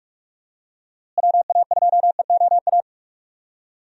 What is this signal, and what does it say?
Morse code, heard as a keyed tone, about 46 wpm, 700 Hz. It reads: WA2EOA
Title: USA Call Signs - 6 Characters - Repeated 3x - 45wpm